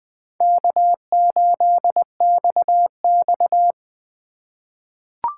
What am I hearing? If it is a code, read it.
Morse code, heard as a keyed tone, about 20 wpm, 700 Hz. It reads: K8XX
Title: USA Call Signs - 4 Characters - 2x Character Spacing - 20wpm